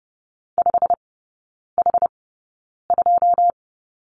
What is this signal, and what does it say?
Morse code, heard as a keyed tone, about 30 wpm, spaced as for 9 wpm, 700 Hz. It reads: 5H2
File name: course / 30:10wpm / 087 Lesson 15e - Number 5 - Three Characters 30wpm